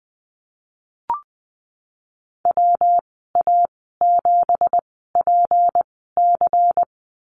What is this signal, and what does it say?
Morse code, heard as a keyed tone, about 20 wpm, spaced as for 14 wpm, 700 Hz. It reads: WA7PC
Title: 186 Lesson 30g - Letter b - Call Signs 20wpm